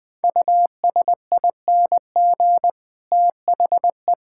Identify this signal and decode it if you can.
Morse code, heard as a keyed tone, about 19 wpm, 700 Hz. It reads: USING THE
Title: All-Age Appropriate Jokes - 20wpm